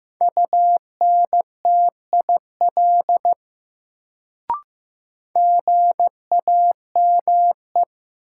Morse code, heard as a keyed tone, about 15 wpm, 700 Hz. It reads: UNTIL GAME
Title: Sentences from Top 500 Words - New Word Drill - 15wpm